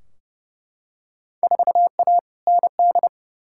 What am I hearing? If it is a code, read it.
Morse code, heard as a keyed tone, about 32 wpm, 700 Hz. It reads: 4A ND
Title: Field Day Exchanges Encore - 30wpm